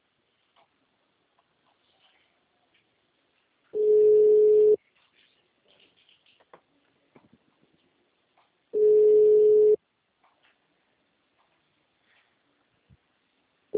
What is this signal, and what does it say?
Ringback tone (Europe, 425 Hz).